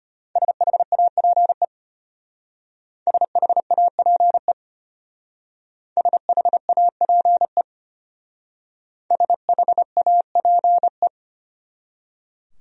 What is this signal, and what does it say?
Morse code, heard as a keyed tone, about 33 wpm, 700 Hz. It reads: SHAPE SHAPE SHAPE SHAPE